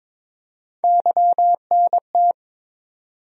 Morse code, heard as a keyed tone, about 22 wpm, 700 Hz. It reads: YNT
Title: USA Call Sign Postfixes - 3 Letters 22wpm